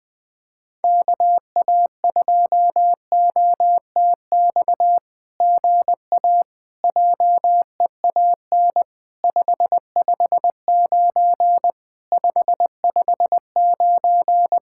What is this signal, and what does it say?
Morse code, heard as a keyed tone, about 20 wpm, 700 Hz. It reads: KA2OTX GA JEAN 559 559
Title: SOTA QSO Practice 20wpm